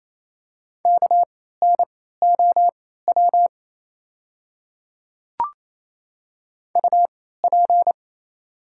Morse code, heard as a keyed tone, about 29 wpm, spaced as for 16 wpm, 700 Hz. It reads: KNOW UP
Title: 3x Character Spacing - Top 200 Words - 28wpm